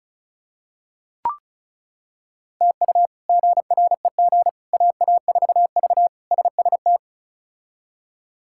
Morse code, heard as a keyed tone, about 36 wpm, 700 Hz. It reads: TU GREG AA4V SST
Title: K1USN SST QSO Practice - 35wpm